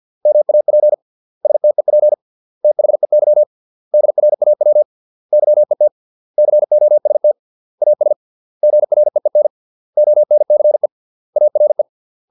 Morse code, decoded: MAP STEP THEY DRAW YET COST AS GREEN ONCE ARE

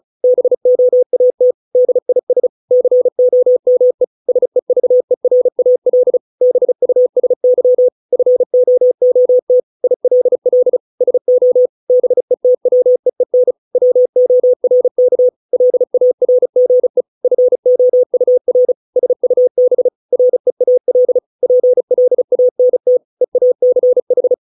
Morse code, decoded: BOAT DIS COME SEVERAL BUSY FOOT ILL SO BETWEEN WORK LARGE FOUR SUB REAL PLANT EACH

